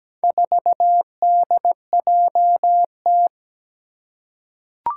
Morse code, heard as a keyed tone, about 17 wpm, 700 Hz. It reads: 4DJT
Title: Four Challenging Letter-Number Combinations 17wpm